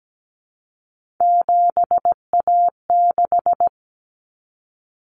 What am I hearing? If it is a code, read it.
Morse code, heard as a keyed tone, about 17 wpm, 700 Hz. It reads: 7A6
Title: Common Cut Numbers - 3-Digit 17wpm